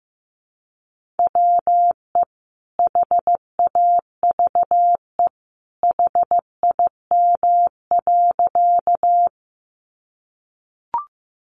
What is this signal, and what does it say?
Morse code, heard as a keyed tone, about 15 wpm, 700 Hz. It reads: WE HAVE HIM.
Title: Sentences from Top 100 Words 15wpm